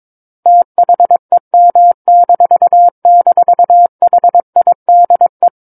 Morse code, heard as a keyed tone, about 22 wpm, 700 Hz. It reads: THEM--HIDE
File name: Book One / 13-Chapter 13 - 22 WPM